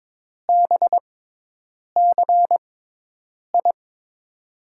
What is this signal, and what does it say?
Morse code, heard as a keyed tone, about 21 wpm, spaced as for 8 wpm, 700 Hz. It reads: BCI
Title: USA Call Signs - 6 Characters - 6x Character Spacing - 22wpm